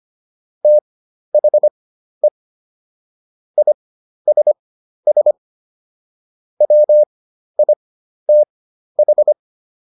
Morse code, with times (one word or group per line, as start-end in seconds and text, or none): 0.6-2.3 THE
3.6-5.3 ISS
6.6-9.3 WITH